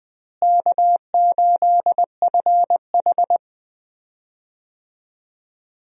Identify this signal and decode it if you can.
Morse code, heard as a keyed tone, about 21 wpm, 690 Hz. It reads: K8FH
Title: USA Call Signs - 4 Characters - Repeated 3x - 20wpm